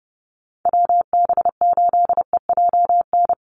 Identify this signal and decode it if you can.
Morse code, heard as a keyed tone, about 30 wpm, 700 Hz. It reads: WB8EJN
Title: Call Signs Repeated 3x - 30wpm